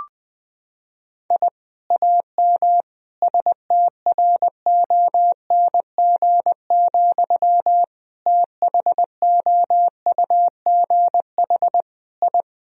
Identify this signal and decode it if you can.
Morse code, heard as a keyed tone, about 20 wpm, 700 Hz. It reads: I AM STRONG, THOUGH I
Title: Sentences from Top 1000 Words - Review Part 3 - 20wpm